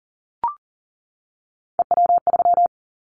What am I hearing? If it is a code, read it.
Morse code, heard as a keyed tone, about 38 wpm, 700 Hz. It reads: EW3
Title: Callsign Prefixes for Top 100 DXCC - 40wpm